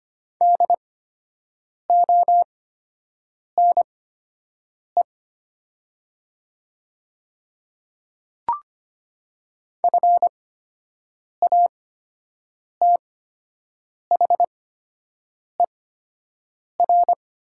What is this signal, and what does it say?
Morse code, heard as a keyed tone, about 25 wpm, spaced as for 7 wpm, 700 Hz. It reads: DONE FATHER
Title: 8x Character Spacing - Top 500 Words - 25wpm